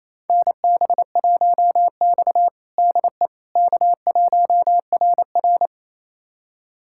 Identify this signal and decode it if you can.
Morse code, heard as a keyed tone, about 28 wpm, 700 Hz. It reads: NB1X DE K1RR
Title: USA Callsign DE Callsign - 4 Characters - 28wpm